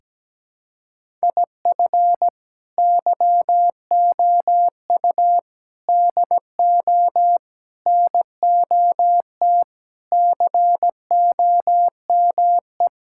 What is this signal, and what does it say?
Morse code, heard as a keyed tone, about 17 wpm, 700 Hz. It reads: IF YOU DO NOT COME
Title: Sentences from Top 1000 Words - Review Part 5 - 17wpm